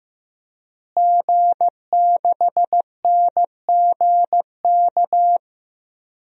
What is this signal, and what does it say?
Morse code, heard as a keyed tone, about 15 wpm, 700 Hz. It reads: G6NGK
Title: DXCC-100 Call Signs - 5 Characters - Repeated 3x - 15wpm